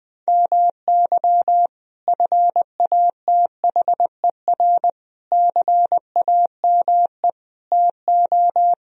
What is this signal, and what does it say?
Morse code, heard as a keyed tone, about 21 wpm, 700 Hz. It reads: MY FATHER CAME TO
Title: Sentences from Top 1000 Words - Review Part 1 - 20wpm